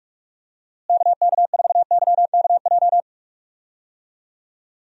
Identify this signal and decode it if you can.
Morse code, heard as a keyed tone, about 43 wpm, 700 Hz. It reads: KK4YKJ
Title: USA Call Signs - 6 Characters - Repeated 2x - 45wpm